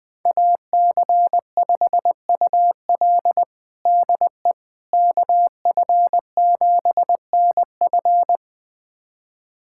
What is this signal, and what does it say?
Morse code, heard as a keyed tone, about 20 wpm, 700 Hz. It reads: AC5UL DE KF7NF
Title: USA Callsign DE Callsign 20wpm